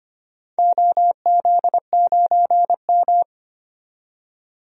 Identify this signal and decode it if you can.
Morse code, heard as a keyed tone, about 25 wpm, 700 Hz. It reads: OZ9M
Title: DXCC-100 Call Signs - 4 Characters - 25wpm